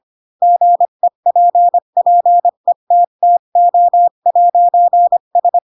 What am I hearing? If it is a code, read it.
Morse code, heard as a keyed tone, about 25 wpm, 700 Hz. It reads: GEPPETTO'S